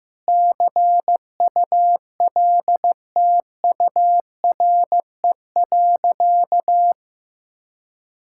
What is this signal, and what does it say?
Morse code, heard as a keyed tone, about 15 wpm, 700 Hz. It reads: CULTURE.
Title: Sentences from Top 1000 Words - Review Part 6 - 15wpm